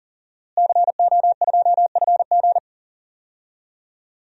Morse code, heard as a keyed tone, about 40 wpm, 700 Hz. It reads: CO2FG
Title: DXCC-100 Call Signs - 5 Characters - Repeated 2x - 40wpm